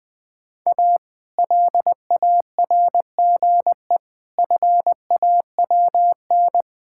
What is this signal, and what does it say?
Morse code, heard as a keyed tone, about 20 wpm, 700 Hz. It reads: A LARGE FAWN
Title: Pangram Challenge v2 - 20wpm